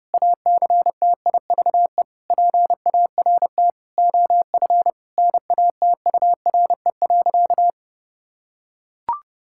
Morse code, heard as a keyed tone, about 30 wpm, 700 Hz. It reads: ACTIVE PART OF NATURE.
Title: Sentences from Top 1000 Words - Review Part 6 - 30wpm